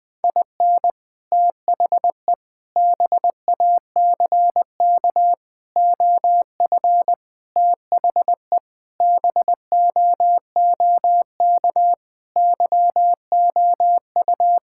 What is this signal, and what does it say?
Morse code, heard as a keyed tone, about 20 wpm, 700 Hz. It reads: IN THE BACK OF THE BOOK YOU
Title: Sentences with 5-Word Phrases - Spoken Beginning - 20wpm